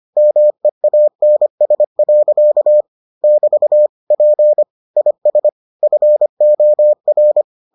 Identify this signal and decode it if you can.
Morse code, heard as a keyed tone, about 25 wpm, 600 Hz. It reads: MEANS. <BT> P IS FOR